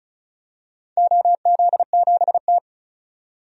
Morse code, heard as a keyed tone, about 36 wpm, 700 Hz. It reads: OZ7T
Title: DXCC-100 Call Signs - 4 Characters - 35wpm